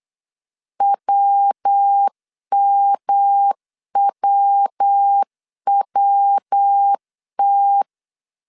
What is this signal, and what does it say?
Morse code, heard as a keyed tone, about 8 wpm, 800 Hz. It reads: WMWWT
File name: CW / DET / 1 3 1500C S A 8wpm 4229